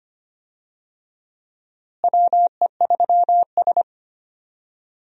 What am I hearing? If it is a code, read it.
Morse code, heard as a keyed tone, about 25 wpm, 700 Hz. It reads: WE3S
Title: USA Call Signs - 4 Characters - Repeated 2x - 25wpm